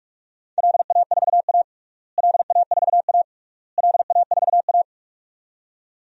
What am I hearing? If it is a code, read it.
Morse code, heard as a keyed tone, about 46 wpm, 700 Hz. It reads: LA4A LA4A LA4A
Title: DXCC-100 Call Signs - 4 Characters - Repeated 3x - 45wpm